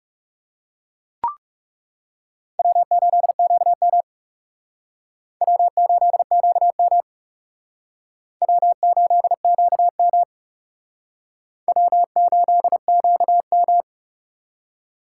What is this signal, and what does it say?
Morse code, heard as a keyed tone, about 38 wpm, 700 Hz. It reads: W8QM W8QM W8QM W8QM
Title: Speed Racing - USA 1x2 Call Signs 30wpm